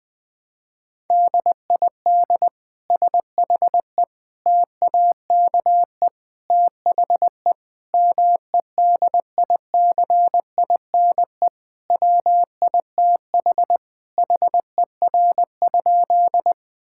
Morse code, heard as a keyed tone, about 20 wpm, 700 Hz. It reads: DID SHE TAKE THE MEDICINE WITH HER?
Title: Sentences from Top 1000 Words - Review Part 6 - 20wpm